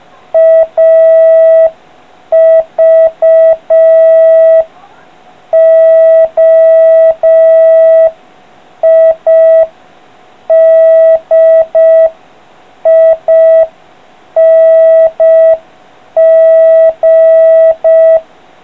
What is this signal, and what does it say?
Morse code, heard as a keyed tone, about 5 wpm, 650 Hz. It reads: AVOIDING